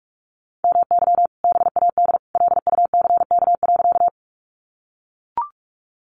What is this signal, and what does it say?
Morse code, heard as a keyed tone, about 44 wpm, 700 Hz. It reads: MY BAD LUCK.
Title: Sentences from Top 1500 Words 45wpm